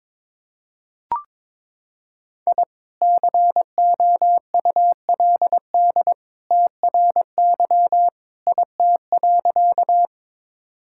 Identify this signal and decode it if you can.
Morse code, heard as a keyed tone, about 22 wpm, 700 Hz. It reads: I COULD TRY IT.
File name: practice / 22wpm / Sentences with 2-Word Phrases - 22wpm